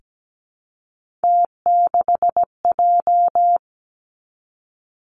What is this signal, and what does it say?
Morse code, heard as a keyed tone, about 17 wpm, 700 Hz. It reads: T6J
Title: Three Challenging Letter-Number Combinations 17wpm